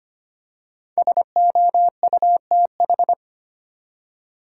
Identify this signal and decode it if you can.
Morse code, heard as a keyed tone, about 25 wpm, 700 Hz. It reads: SOUTH
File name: 5 Letter Words - First 3 Letters Given - 25wpm